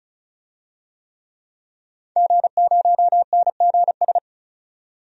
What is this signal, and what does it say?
Morse code, heard as a keyed tone, about 36 wpm, 700 Hz. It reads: G0NGS
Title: DXCC-100 Call Signs - 5 Characters - 35wpm